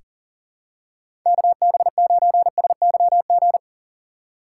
Morse code, heard as a keyed tone, about 40 wpm, 700 Hz. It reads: KB9SYG
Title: USA Call Signs - 6 Characters - 6x Character Spacing - 40wpm